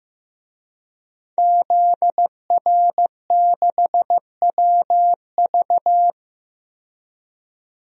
Morse code, heard as a keyed tone, about 15 wpm, 700 Hz. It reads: ZR6WV
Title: DXCC-100 Call Signs - 5 Characters - 15wpm